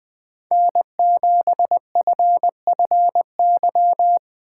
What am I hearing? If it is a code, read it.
Morse code, heard as a keyed tone, about 20 wpm, 700 Hz. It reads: N7FFY
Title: USA Call Signs - 5 Characters - 2x Character Spacing - 20wpm